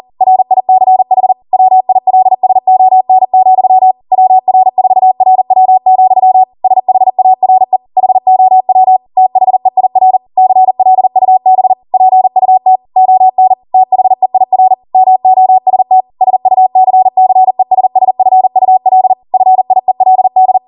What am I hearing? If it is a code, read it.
Morse code, heard as a keyed tone, about 40 wpm, 750 Hz. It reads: RICH WILSON, WR4RW, SHARE HOW THEIR CLUB PUT ON THEIR MOST SUCCESSFUL FIELD